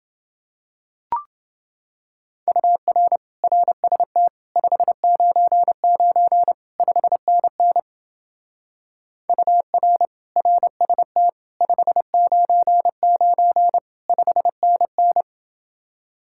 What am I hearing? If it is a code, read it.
Morse code, heard as a keyed tone, about 29 wpm, 700 Hz. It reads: UR RST 599 5NN UR RST 599 5NN